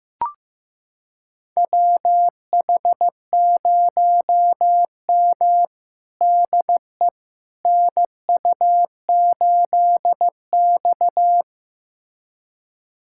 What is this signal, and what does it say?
Morse code, heard as a keyed tone, about 15 wpm, 700 Hz. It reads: WH0M DE NU8X
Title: USA Callsign DE Callsign - 4 Characters - 15wpm